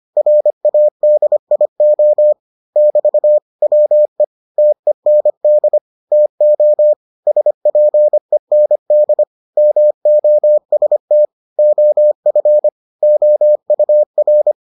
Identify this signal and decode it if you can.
Morse code, heard as a keyed tone, about 25 wpm, 600 Hz. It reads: RADIO <BT> WE TEND TO SPEND MOST OF OUR